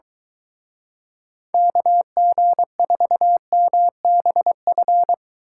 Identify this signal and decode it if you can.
Morse code, heard as a keyed tone, about 23 wpm, 700 Hz. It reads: KG4MBF